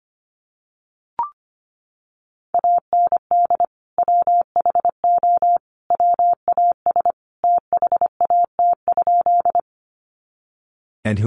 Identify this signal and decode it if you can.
Morse code, heard as a keyed tone, about 25 wpm, 700 Hz. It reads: AND WHO WAS THAT?
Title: Sentences from Top 100 Words 25wpm